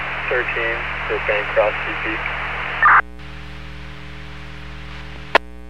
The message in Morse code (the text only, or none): none